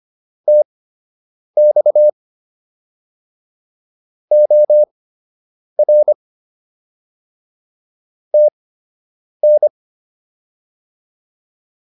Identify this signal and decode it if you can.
Morse code, heard as a keyed tone, about 25 wpm, spaced as for 8 wpm, 600 Hz. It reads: TX OR TN